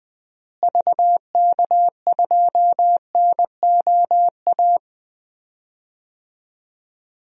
Morse code, heard as a keyed tone, about 21 wpm, 700 Hz. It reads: VK2NOA
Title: DXCC-100 Call Signs - 6 Characters - Repeated 2x - 20wpm